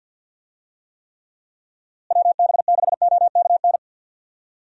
Morse code, heard as a keyed tone, about 51 wpm, 700 Hz. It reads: WB6OKN
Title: USA Call Signs - 6 Characters - Repeated 3x - 50wpm